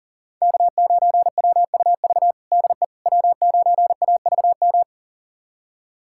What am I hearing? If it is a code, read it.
Morse code, heard as a keyed tone, about 37 wpm, 700 Hz. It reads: K9WUV DE W9AVM
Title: USA Callsign DE Callsign - 5 Characters - 40wpm